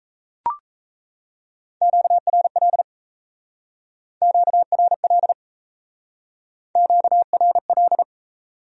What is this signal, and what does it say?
Morse code, heard as a keyed tone, about 34 wpm, 700 Hz. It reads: QRL QRL QRL
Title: Speed-Racing Q-Codes - 28wpm